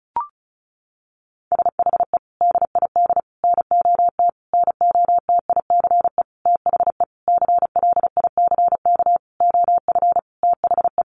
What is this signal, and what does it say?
Morse code, heard as a keyed tone, about 34 wpm, 700 Hz. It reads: SHE DID NOT NOTICE THE CLICK OF THE
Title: Sentences from Top 1100 Words 35wpm